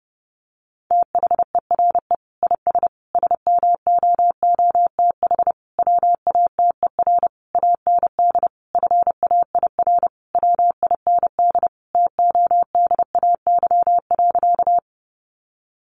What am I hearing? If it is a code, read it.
Morse code, heard as a keyed tone, about 30 wpm, 700 Hz. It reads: THERE IS SMOOTH WATER AND FAIR WIND TODAY.
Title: Sentences from Top 1400 Words 30wpm